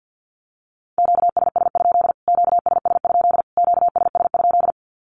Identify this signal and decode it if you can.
Morse code, heard as a keyed tone, about 50 wpm, 700 Hz. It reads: QSS? QSS? QSS?
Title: Q-Codes Repeated 3x 50wpm